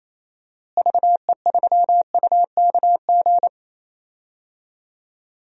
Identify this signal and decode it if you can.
Morse code, heard as a keyed tone, about 28 wpm, 700 Hz. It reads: VE3UKG
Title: DXCC-100 Call Signs - 6 Characters - Repeated 3x - 28wpm